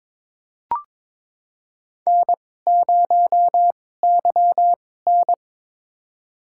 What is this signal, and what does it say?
Morse code, heard as a keyed tone, about 21 wpm, spaced as for 16 wpm, 700 Hz. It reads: N0YN
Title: USA Call Signs - 2x Character Spacing - 22wpm